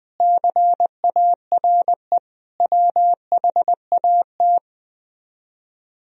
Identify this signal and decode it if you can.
Morse code, heard as a keyed tone, about 20 wpm, 700 Hz. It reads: CARE WHAT